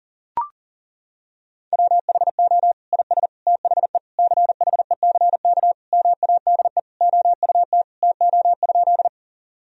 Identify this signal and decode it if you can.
Morse code, heard as a keyed tone, about 40 wpm, 700 Hz. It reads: WHO IS THE CHECK MADE OUT TO?
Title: Sentences from Top 1000 Words - Review Part 3 - 40wpm